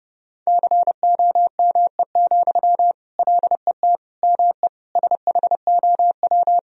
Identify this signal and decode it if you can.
Morse code, heard as a keyed tone, about 32 wpm, 700 Hz. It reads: COME, LET ME SHOW
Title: Sentences from Top 1000 Words - Review Part 2 - 30wpm